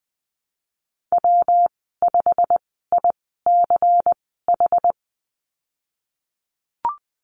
Morse code, heard as a keyed tone, about 20 wpm, spaced as for 15 wpm, 700 Hz. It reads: W5ICH
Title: 179 Lesson 29g - Slash - Call Signs 20wpm